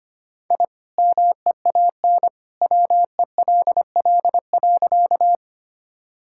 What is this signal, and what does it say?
Morse code, heard as a keyed tone, about 25 wpm, 700 Hz. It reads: I MEAN WELL.
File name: practice / 25wpm / Sentences with 2-Word Phrases - 25wpm